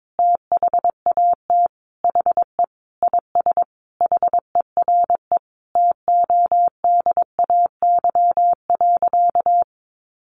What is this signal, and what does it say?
Morse code, heard as a keyed tone, about 22 wpm, 700 Hz. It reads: THAT HE IS HERE TODAY.
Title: Sentences from Top 1100 Words 22wpm